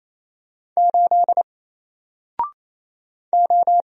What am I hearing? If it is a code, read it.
Morse code, heard as a keyed tone, about 28 wpm, 700 Hz. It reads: 8 O
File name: Single-Letter-Number-28wpm